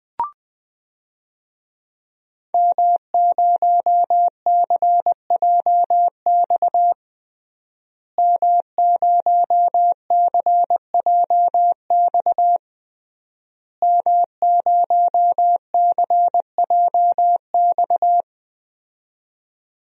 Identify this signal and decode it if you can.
Morse code, heard as a keyed tone, about 19 wpm, 700 Hz. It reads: M0CJX M0CJX M0CJX
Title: DXCC-100 Call Signs - 5 Characters - Repeated 3x - 20wpm